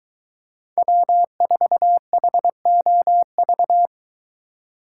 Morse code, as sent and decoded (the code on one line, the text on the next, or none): .-- ....- .... --- ...-
W4HOV